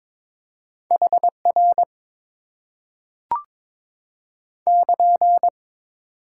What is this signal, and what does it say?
Morse code, heard as a keyed tone, about 22 wpm, 700 Hz. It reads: HR <KN>
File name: Top 90 QSO Elements - 8x Character Spacing - 22wpm